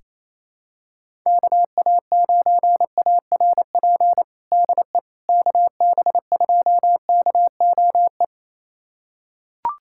Morse code, heard as a keyed tone, about 27 wpm, 700 Hz. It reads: KA9ARP DE KB2KOE
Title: USA Callsign DE Callsign - 6 Characters - 28wpm